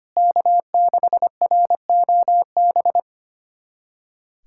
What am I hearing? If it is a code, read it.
Morse code, heard as a keyed tone, about 25 wpm, 700 Hz. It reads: K6ROB
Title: USA Callsign DE Callsign - 5 Characters - 25wpm